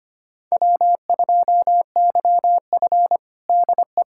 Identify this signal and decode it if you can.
Morse code, heard as a keyed tone, about 25 wpm, 700 Hz. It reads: W2YF DE
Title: USA Callsign DE Callsign - 4 Characters - 25wpm